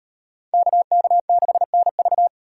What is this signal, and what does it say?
Morse code, heard as a keyed tone, about 36 wpm, 700 Hz. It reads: KK6NV